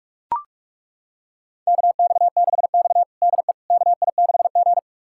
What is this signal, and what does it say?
Morse code, heard as a keyed tone, about 42 wpm, 700 Hz. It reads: KX6X DE KI6G
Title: USA Callsign DE Callsign - 4 Characters - 45wpm